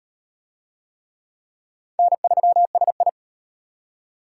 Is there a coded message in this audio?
Yes: Morse code from 2.0 to 3.1 s, heard as a keyed tone, about 34 wpm, 700 Hz. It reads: N3SI